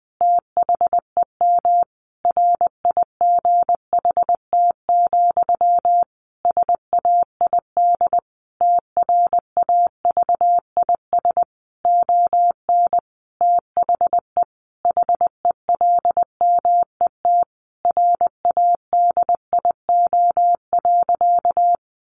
Morse code, heard as a keyed tone, about 20 wpm, 700 Hz. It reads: THEM RIGHT, SAID TRAVIS ON THE HELMET RADIO.
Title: A Sound Of Thunder 20wpm